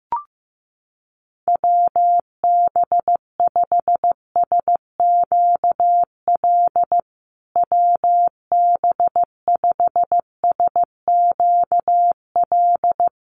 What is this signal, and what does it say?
Morse code, heard as a keyed tone, about 15 wpm, 700 Hz. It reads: WB5SQL WB5SQL
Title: USA Call Signs Repeated 3x 15wpm